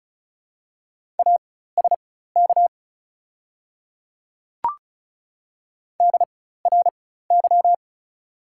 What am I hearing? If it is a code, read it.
Morse code, heard as a keyed tone, about 36 wpm, spaced as for 16 wpm, 690 Hz. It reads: ASK DRY
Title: 4x Character Spacing - Top 500 Words - 35wpm